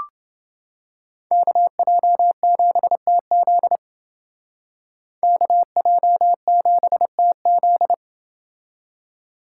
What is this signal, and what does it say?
Morse code, heard as a keyed tone, about 28 wpm, 700 Hz. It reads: KJ7TZ KJ7TZ